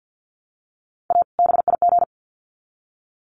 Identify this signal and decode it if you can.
Morse code, heard as a keyed tone, about 50 wpm, 700 Hz. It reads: A BIG
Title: Sets of 2 Words 50wpm